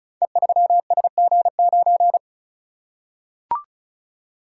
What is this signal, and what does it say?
Morse code, heard as a keyed tone, about 36 wpm, 700 Hz. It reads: E3SG9
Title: Code Groups - 5 Letter:Numbers - 35wpm